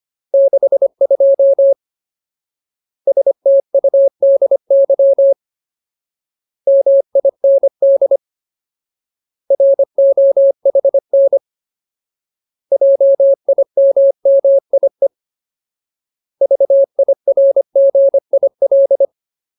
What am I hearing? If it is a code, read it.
Morse code, heard as a keyed tone, about 25 wpm, 550 Hz. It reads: 62 STUDY MIND ROHN JIMMIE VIRGIL
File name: Ultimate Word List / 25WPM Ultimate